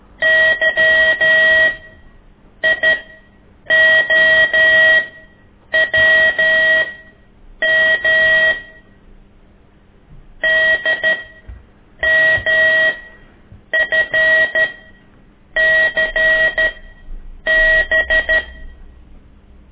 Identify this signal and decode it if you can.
Morse code, heard as a keyed tone, about 12 wpm, spaced as for 7 wpm, 1900 Hz. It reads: YIOWM DMFCB
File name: ik1weg2 / CORSI CW